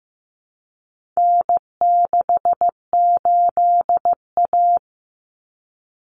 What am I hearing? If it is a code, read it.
Morse code, heard as a keyed tone, about 15 wpm, 700 Hz. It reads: N68A
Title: Common Cut Numbers - 4-Digit 15wpm